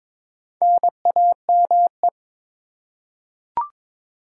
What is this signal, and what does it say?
Morse code, heard as a keyed tone, about 22 wpm, 700 Hz. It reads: NAME
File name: Top 20 QSO Elements - 8x Character Spacing - 22wpm